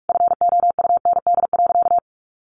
Morse code, heard as a keyed tone, about 44 wpm, 700 Hz. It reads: FOUND.
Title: Sentences from Top 1500 Words 45wpm